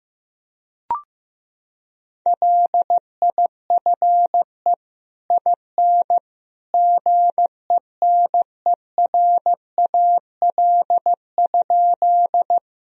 Morse code, heard as a keyed tone, about 15 wpm, 700 Hz. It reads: LIFE IN GENERAL?
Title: Sentences from Top 1000 Words - Review Part 3 - 15wpm